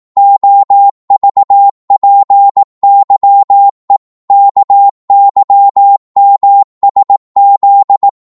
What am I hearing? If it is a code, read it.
Morse code, heard as a keyed tone, about 18 wpm, 800 Hz. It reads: OVPYE KYMSZ